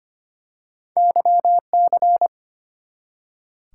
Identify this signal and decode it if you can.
Morse code, heard as a keyed tone, about 25 wpm, 700 Hz. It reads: YC
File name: Two Letter-Number Combinations - 25wpm